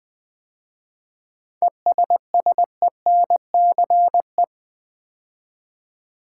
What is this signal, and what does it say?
Morse code, heard as a keyed tone, about 20 wpm, 700 Hz. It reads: ESSENCE
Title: 7 Letter Words - First 3 Letters Given - 20wpm